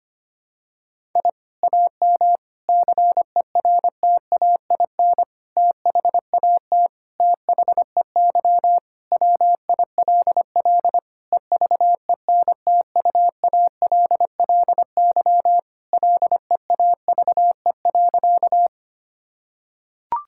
Morse code, heard as a keyed tone, about 25 wpm, 700 Hz. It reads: I AM CERTAIN THAT THEY WILL EVENTUALLY LEAVE.
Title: Sentences from Top 1000 Words - Review Part 6 - 25wpm